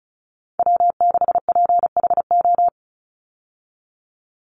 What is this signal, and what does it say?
Morse code, heard as a keyed tone, about 34 wpm, 700 Hz. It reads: W6PHO
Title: Call Signs Repeated 2x - 35wpm